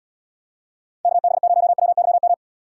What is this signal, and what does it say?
Morse code, heard as a keyed tone, about 74 wpm, 700 Hz. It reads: KB0WYM